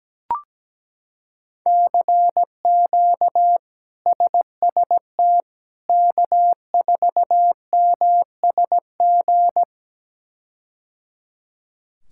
Morse code, heard as a keyed tone, about 17 wpm, 700 Hz. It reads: CQ SST K4MSG
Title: K1USN SST QSO Practice - 17wpm